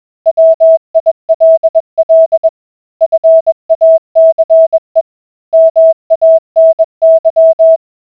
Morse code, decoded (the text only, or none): WILL FACE MANY